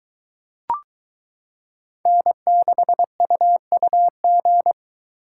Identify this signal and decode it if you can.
Morse code, heard as a keyed tone, about 23 wpm, 700 Hz. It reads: N6UUG